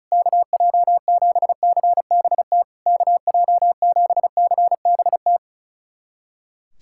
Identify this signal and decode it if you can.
Morse code, heard as a keyed tone, about 36 wpm, 700 Hz. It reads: KJ7CBT KJ7CBT
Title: SOTA QSO Practice 35wpm